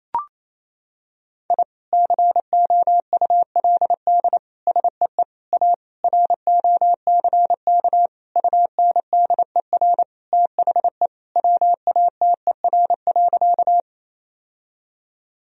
Morse code, heard as a keyed tone, about 28 wpm, 700 Hz. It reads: I COULD SEE A ROCK UNDER THE WATER.
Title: Sentences from Top 1000 Words - Review Part 2 - 28wpm